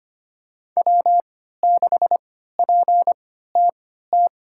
Morse code, heard as a keyed tone, about 25 wpm, spaced as for 14 wpm, 700 Hz. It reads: W6PTT
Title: USA Call Signs - 5 Characters - 3x Character Spacing - 25wpm